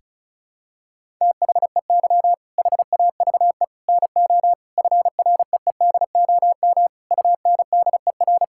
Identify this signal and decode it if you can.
Morse code, heard as a keyed tone, about 34 wpm, 700 Hz. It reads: THEY HAVE NO FREEDOM UNDER
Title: Sentences from Top 1000 Words - Review Part 5 - 35wpm